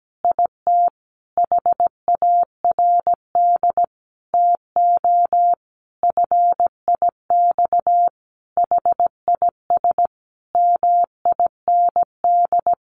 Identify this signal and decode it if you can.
Morse code, heard as a keyed tone, about 17 wpm, 700 Hz. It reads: IT HARD TO FIX HIS MIND
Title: Sentences from Top 1400 Words 17wpm